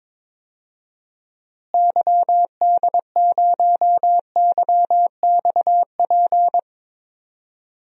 Morse code, heard as a keyed tone, about 22 wpm, 700 Hz. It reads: YD0YXP
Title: DXCC-100 Call Signs - 6 Characters - Repeated 2x - 22wpm